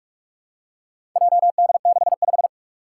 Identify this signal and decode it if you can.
Morse code, heard as a keyed tone, about 41 wpm, 700 Hz. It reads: JD65
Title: Four Challenging Letter-Number Combinations 45wpm